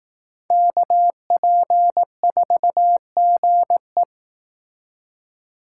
Morse code, heard as a keyed tone, about 18 wpm, 700 Hz. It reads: KP4GE
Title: Speed Racing - USA 2x2 Call Signs 15wpm